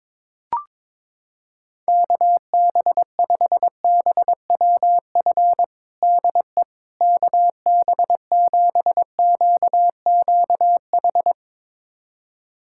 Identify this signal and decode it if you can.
Morse code, heard as a keyed tone, about 22 wpm, 700 Hz. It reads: KB5BWF DE KB7QQH
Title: USA Callsign DE Callsign 22wpm